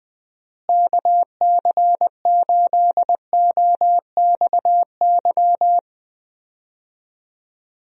Morse code, heard as a keyed tone, about 20 wpm, 700 Hz. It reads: KC8OXY
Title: USA Call Signs - 6 Characters - Repeated 3x - 20wpm